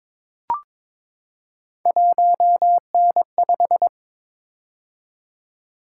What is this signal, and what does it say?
Morse code, heard as a keyed tone, about 22 wpm, 700 Hz. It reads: 1N5
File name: practice / 22wpm / Three Letter-Number Combinations - 22wpm